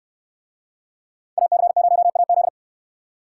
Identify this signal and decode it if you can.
Morse code, heard as a keyed tone, about 62 wpm, 700 Hz. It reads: AC0AZ